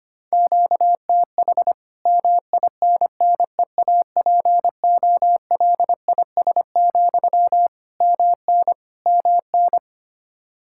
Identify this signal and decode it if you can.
Morse code, heard as a keyed tone, about 25 wpm, 700 Hz. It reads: QTH MINNEAPOLIS, MN MN
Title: Practice QSO - Exchange 1 - Beginning - 25wpm